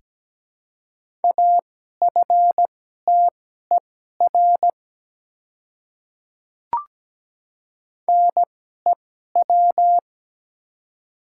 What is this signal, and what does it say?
Morse code, heard as a keyed tone, about 17 wpm, spaced as for 12 wpm, 690 Hz. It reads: AFTER NEW
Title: 2x Character Spacing - Top 300 Words - 17wpm